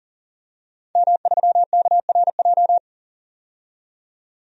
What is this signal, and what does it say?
Morse code, heard as a keyed tone, about 40 wpm, 700 Hz. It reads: M3KRJ
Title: DXCC-100 Call Signs - 5 Characters - 40wpm